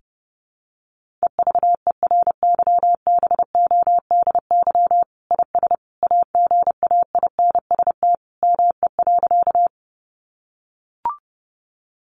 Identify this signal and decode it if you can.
Morse code, heard as a keyed tone, about 30 wpm, 700 Hz. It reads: EVERYBODY IS AGAINST ME.
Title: Sentences from Top 1400 Words 30wpm